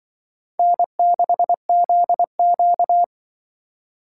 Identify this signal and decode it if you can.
Morse code, heard as a keyed tone, about 24 wpm, 700 Hz. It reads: N6ZQ
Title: Speed Racing - USA 1x2 Call Signs 20wpm